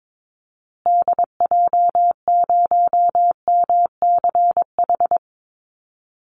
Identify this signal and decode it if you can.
Morse code, heard as a keyed tone, about 22 wpm, 700 Hz. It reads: DJ0MCH
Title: Call Signs Repeated 3x - 22wpm